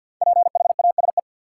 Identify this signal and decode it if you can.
Morse code, heard as a keyed tone, about 53 wpm, 700 Hz. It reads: PHASE
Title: Sentences from Top 1200 Words - New Word Drill - 50wpm